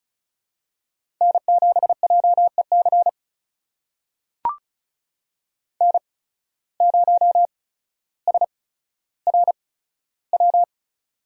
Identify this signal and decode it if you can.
Morse code, heard as a keyed tone, about 36 wpm, spaced as for 15 wpm, 700 Hz. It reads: N7JEC N0SRW
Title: USA Call Signs - 5 Characters - 8x Character Spacing - 35wpm